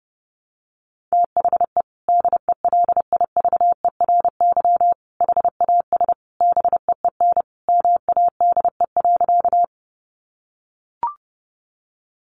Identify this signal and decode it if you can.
Morse code, heard as a keyed tone, about 30 wpm, 700 Hz. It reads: THE DELIVERY HAS BEEN MADE.
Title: Sentences from Top 1500 Words 30wpm